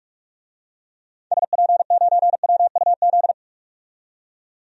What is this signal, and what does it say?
Morse code, heard as a keyed tone, about 44 wpm, 700 Hz. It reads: SP9WUZ
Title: DXCC-100 Call Signs - 6 Characters - 45wpm